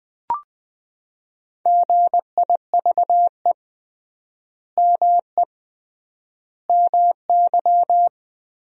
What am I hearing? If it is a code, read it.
Morse code, heard as a keyed tone, about 21 wpm, 700 Hz. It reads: GIVE ME MY
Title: 3x Word-Spacing - Sentences from Top 200 Words - 20wpm